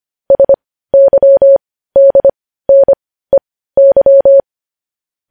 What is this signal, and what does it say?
Morse code, heard as a keyed tone, about 25 wpm, spaced as for 15 wpm, 550 Hz. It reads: SYDNEY
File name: podcast / News headlines-1x-25x15